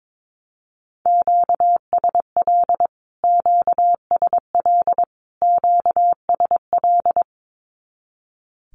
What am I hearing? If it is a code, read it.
Morse code, heard as a keyed tone, about 22 wpm, 700 Hz. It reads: QSL QSL QSL
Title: Q-Codes Repeated 3x 22wpm